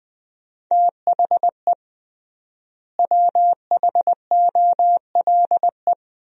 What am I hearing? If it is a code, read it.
Morse code, heard as a keyed tone, about 21 wpm, 700 Hz. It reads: THE WHOLE
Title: 3x Word-Spacing - Sentences from Top 500 Words - 20wpm